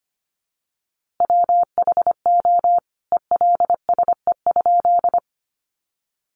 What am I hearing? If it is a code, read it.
Morse code, heard as a keyed tone, about 25 wpm, 700 Hz. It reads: WHO ELSE?
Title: A Sound Of Thunder 25wpm